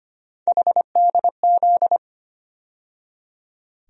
Morse code, heard as a keyed tone, about 25 wpm, 700 Hz. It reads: HDZ
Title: USA Call Sign Postfixes - 3 Letters 25wpm